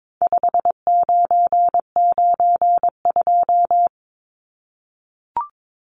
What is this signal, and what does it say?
Morse code, heard as a keyed tone, about 22 wpm, 700 Hz. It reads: 5992
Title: Numbers - 4-Digit 22wpm